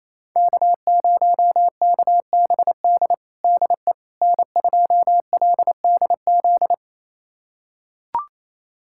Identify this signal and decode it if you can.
Morse code, heard as a keyed tone, about 28 wpm, 700 Hz. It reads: K0KBD DE N2LDZ
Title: USA Callsign DE Callsign - 5 Characters - 28wpm